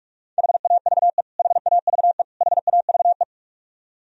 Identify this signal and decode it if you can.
Morse code, heard as a keyed tone, about 46 wpm, 700 Hz. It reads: HAVE HAVE HAVE
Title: Top 100 Words Repeated 45wpm